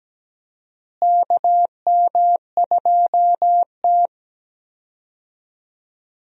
Morse code, heard as a keyed tone, about 17 wpm, 700 Hz. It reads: KM2T
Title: USA Call Signs - 4 Characters - Repeated 2x - 17wpm